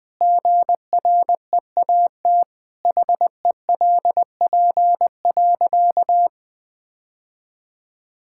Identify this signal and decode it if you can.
Morse code, heard as a keyed tone, about 21 wpm, 700 Hz. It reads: GREAT HELP.
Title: Sentences with 3-Word Phrases - 20wpm